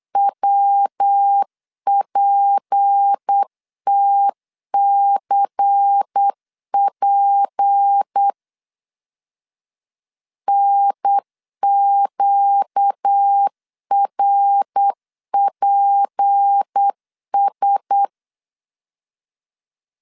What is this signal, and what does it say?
Morse code, heard as a keyed tone, about 8 wpm, 800 Hz. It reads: WPTCP NQRPS
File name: LEZIONI / 6 PKX0 / 6 PK 8wpm 3003